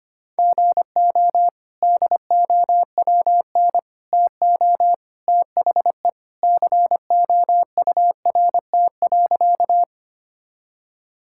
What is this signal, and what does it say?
Morse code, heard as a keyed tone, about 25 wpm, 700 Hz. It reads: GO DOWN TO THE COURT.
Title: Sentences from Top 1600 Words 25wpm